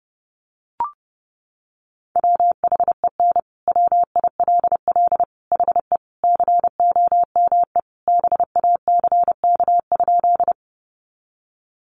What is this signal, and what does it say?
Morse code, heard as a keyed tone, about 30 wpm, 700 Hz. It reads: WHEN WILL HE COME BACK?
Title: Sentences from Top 200 Words 30wpm